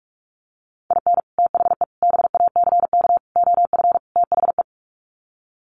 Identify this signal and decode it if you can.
Morse code, heard as a keyed tone, about 44 wpm, 700 Hz. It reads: IN THE BACK OF THE
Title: Sets of 5 Words 45wpm